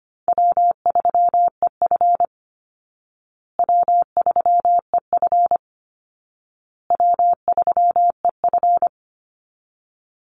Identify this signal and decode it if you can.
Morse code, heard as a keyed tone, about 25 wpm, 700 Hz. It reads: W3EF W3EF W3EF
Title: Easy Call Signs Repeated 3x - 25wpm